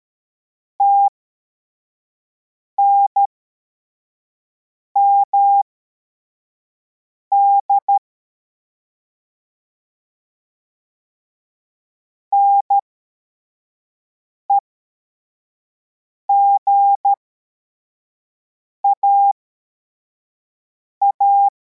Morse code, heard as a keyed tone, about 13 wpm, spaced as for 4 wpm, 800 Hz. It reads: TNMD NEGAA